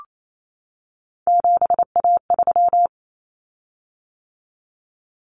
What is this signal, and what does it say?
Morse code, heard as a keyed tone, about 28 wpm, 700 Hz. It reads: 7A3
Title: Common Cut Numbers - 3-Digit 28wpm